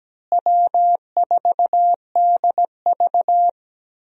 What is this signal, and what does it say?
Morse code, heard as a keyed tone, about 17 wpm, 700 Hz. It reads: W4DV
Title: USA Call Signs - 4 Characters - Repeated 2x - 17wpm